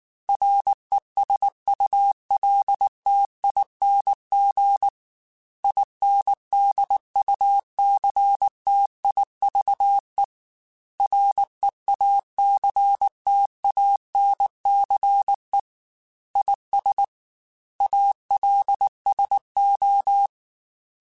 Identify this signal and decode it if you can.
Morse code, heard as a keyed tone, about 19 wpm, 800 Hz. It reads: RESULTING INDUCTIVE REACTANCE IS ALSO